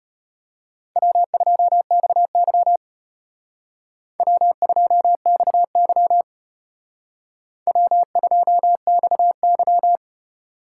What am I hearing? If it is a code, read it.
Morse code, heard as a keyed tone, about 31 wpm, 700 Hz. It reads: W2XY W2XY W2XY
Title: Speed Racing - USA 1x2 Call Signs 25wpm